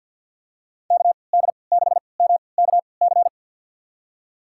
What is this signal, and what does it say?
Morse code, heard as a keyed tone, about 52 wpm, spaced as for 29 wpm, 700 Hz. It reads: KD6MKC
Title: USA Call Signs - 6 Characters - 3x Character Spacing - 50wpm